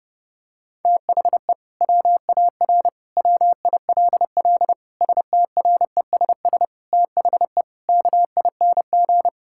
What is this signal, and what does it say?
Morse code, heard as a keyed tone, about 30 wpm, 700 Hz. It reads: THE WAR WILL STRESS THE KING
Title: Sentences from Top 1000 Words - Review Part 6 - 30wpm